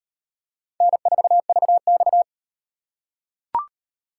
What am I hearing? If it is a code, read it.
Morse code, heard as a keyed tone, about 37 wpm, 700 Hz. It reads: N4VX